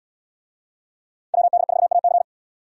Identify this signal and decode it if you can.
Morse code, heard as a keyed tone, about 70 wpm, 700 Hz. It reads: WD4AC